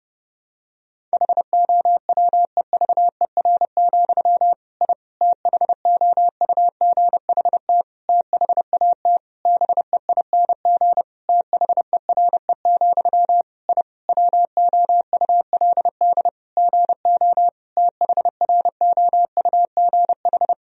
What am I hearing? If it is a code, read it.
Morse code, heard as a keyed tone, about 30 wpm, 700 Hz. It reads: HOWEVER, I THOUGHT THAT BEING THERE, I WOULD GO THROUGH
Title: Sentences from Top 1000 Words - Review Part 3 - 30wpm